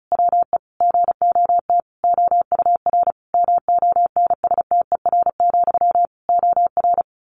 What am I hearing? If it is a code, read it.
Morse code, heard as a keyed tone, about 34 wpm, 700 Hz. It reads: WE GOT OUR MONSTER, OR
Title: A Sound Of Thunder 35wpm